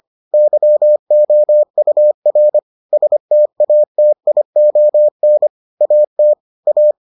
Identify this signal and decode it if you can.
Morse code, heard as a keyed tone, about 25 wpm, 600 Hz. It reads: YOUR STATION AT A